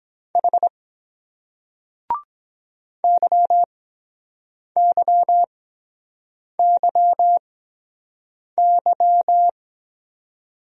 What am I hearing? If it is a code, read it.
Morse code, heard as a keyed tone, about 22 wpm, spaced as for 7 wpm, 700 Hz. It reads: H YYYY